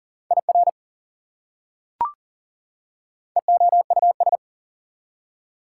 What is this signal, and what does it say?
Morse code, heard as a keyed tone, about 40 wpm, 700 Hz. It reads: IR EOUS
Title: Prefixes and Suffixes 40wpm